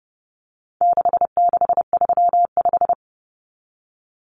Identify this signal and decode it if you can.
Morse code, heard as a keyed tone, about 30 wpm, 700 Hz. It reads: 6635
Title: Numbers - 4-Digit 30wpm